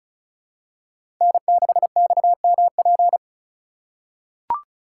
Morse code, heard as a keyed tone, about 34 wpm, 700 Hz. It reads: N6XMP
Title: USA Call Signs - 5 Characters - 3x Character Spacing - 35wpm